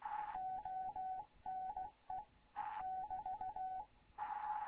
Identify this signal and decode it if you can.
Morse code, heard as a keyed tone, about 16 wpm, 750 Hz, with radio noise. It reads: ONE <BT>
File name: Morze-HA2MN / hangp / 2 11 GYAKORLAT